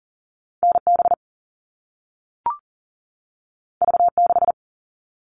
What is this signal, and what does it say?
Morse code, heard as a keyed tone, about 42 wpm, 700 Hz. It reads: NB V6
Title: All Cut Numbers - 2-Digit 40wpm